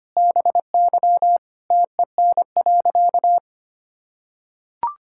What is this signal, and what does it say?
Morse code, heard as a keyed tone, about 25 wpm, 700 Hz. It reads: BY TEN.
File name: practice / 25wpm / Sentences with 4-Word Phrases - 25wpm